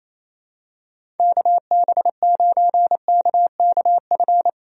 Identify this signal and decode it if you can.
Morse code, heard as a keyed tone, about 28 wpm, 700 Hz. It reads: KB9KKF
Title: USA Call Signs - 6 Characters - Repeated 2x - 28wpm